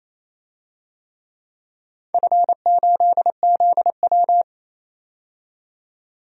Morse code, heard as a keyed tone, about 28 wpm, 700 Hz. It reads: F8ZW
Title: DXCC-100 Call Signs - 4 Characters - 28wpm